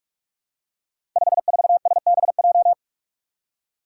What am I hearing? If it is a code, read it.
Morse code, heard as a keyed tone, about 44 wpm, 700 Hz. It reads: 54SBJ
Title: Five Challenging Letter-Number Combinations 45wpm